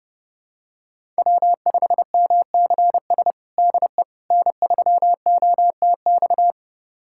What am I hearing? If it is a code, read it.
Morse code, heard as a keyed tone, about 30 wpm, 700 Hz. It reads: W5MCS DE N3OTX
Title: USA Callsign DE Callsign - 5 Characters - 30wpm